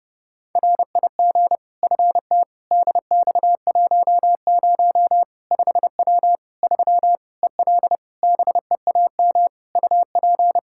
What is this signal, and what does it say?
Morse code, heard as a keyed tone, about 30 wpm, 700 Hz. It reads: RIG FT DX10 5W 3 EL BEAM UP
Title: Practice QSO - Exchange 2 - Info - 30wpm